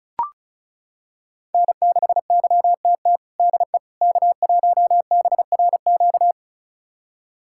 Morse code, heard as a keyed tone, about 36 wpm, 700 Hz. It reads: N6YTT DE K1BRQ
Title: USA Callsign DE Callsign 35wpm